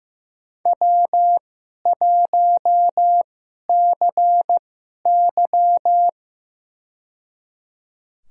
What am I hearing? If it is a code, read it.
Morse code, heard as a keyed tone, about 15 wpm, spaced as for 11 wpm, 700 Hz. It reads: W1CY
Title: USA Call Signs - 4 Characters - 2x Character Spacing - 15wpm